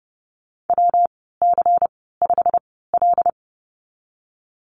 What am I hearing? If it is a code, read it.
Morse code, heard as a keyed tone, about 30 wpm, spaced as for 17 wpm, 700 Hz. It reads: WC5L
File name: Lesson 19g - Letter w - Call Signs 30wpm